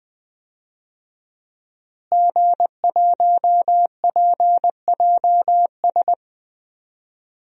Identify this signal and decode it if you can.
Morse code, heard as a keyed tone, about 21 wpm, 700 Hz. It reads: G1PJS
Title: DXCC-100 Call Signs - 5 Characters - Repeated 3x - 20wpm